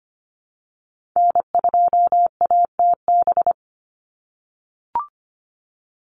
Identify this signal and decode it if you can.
Morse code, heard as a keyed tone, about 25 wpm, 700 Hz. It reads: N2ATB
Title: Call Signs - 25wpm